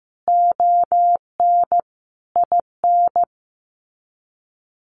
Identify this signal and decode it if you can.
Morse code, heard as a keyed tone, about 15 wpm, 700 Hz. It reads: ON IN
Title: Sets of 5 Words Encore 15wpm